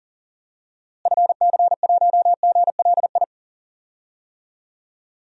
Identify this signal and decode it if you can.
Morse code, heard as a keyed tone, about 38 wpm, 700 Hz. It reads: FC1GLI
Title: DXCC-100 Call Signs - 6 Characters - Repeated 2x - 40wpm